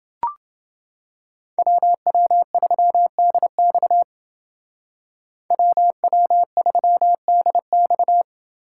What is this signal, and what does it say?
Morse code, heard as a keyed tone, about 29 wpm, 700 Hz. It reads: WW3DX WW3DX